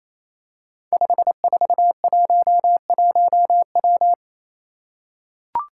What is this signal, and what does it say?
Morse code, heard as a keyed tone, about 28 wpm, 700 Hz. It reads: 5411W
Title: Five Challenging Letter-Number Combinations 28wpm